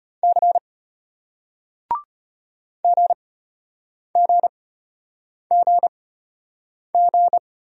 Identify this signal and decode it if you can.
Morse code, heard as a keyed tone, about 34 wpm, spaced as for 8 wpm, 700 Hz. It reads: C GGGG